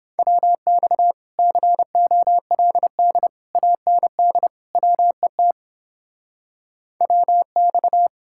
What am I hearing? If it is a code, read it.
Morse code, heard as a keyed tone, about 31 wpm, 700 Hz. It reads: WX COLD AND WET WX